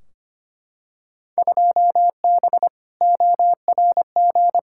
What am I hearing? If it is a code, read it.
Morse code, heard as a keyed tone, about 25 wpm, 700 Hz. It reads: 2B ORG
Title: Field Day Exchanges Encore - 25wpm